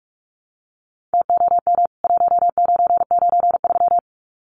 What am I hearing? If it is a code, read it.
Morse code, heard as a keyed tone, about 44 wpm, 700 Hz. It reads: TOM 1993
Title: CWT Exchange 45wpm